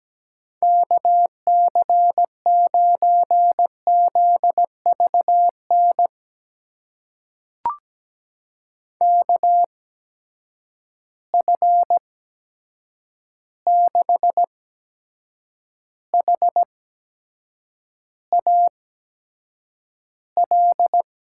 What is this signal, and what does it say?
Morse code, heard as a keyed tone, about 17 wpm, spaced as for 7 wpm, 700 Hz. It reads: KC9ZVN KF6HAL
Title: USA Call Signs - 6 Characters - 8x Character Spacing - 17wpm